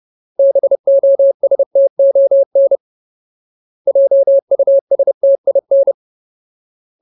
Morse code, heard as a keyed tone, about 30 wpm, 550 Hz. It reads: BOSTON JUSTIN
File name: Ultimate Word List / 30WPM Ultimate